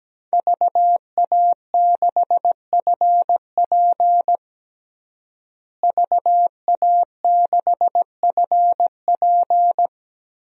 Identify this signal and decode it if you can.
Morse code, heard as a keyed tone, about 17 wpm, 700 Hz. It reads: VA6FP VA6FP
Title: DXCC-100 Call Signs - 5 Characters - Repeated 2x - 17wpm